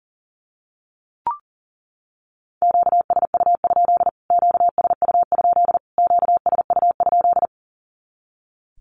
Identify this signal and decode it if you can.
Morse code, heard as a keyed tone, about 40 wpm, 700 Hz. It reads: QSU? QSU? QSU?
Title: Q-Codes Repeated 3x 40wpm